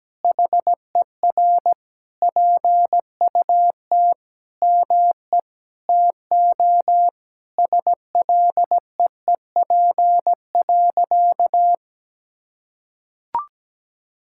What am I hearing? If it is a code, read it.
Morse code, heard as a keyed tone, about 17 wpm, 700 Hz. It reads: HER PUT ME TO SLEEP.
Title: Sentences from Top 1600 Words 17wpm